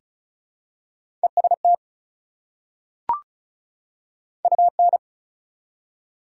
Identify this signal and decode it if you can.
Morse code, heard as a keyed tone, about 36 wpm, 690 Hz. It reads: EST UN
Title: Prefixes and Suffixes 35wpm